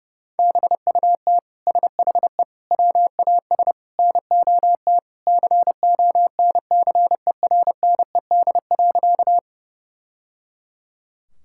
Morse code, heard as a keyed tone, about 30 wpm, 700 Hz. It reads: BUT SHE WAS NOT CONCERNED.
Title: Sentences from Top 1000 Words - Review Part 6 - 30wpm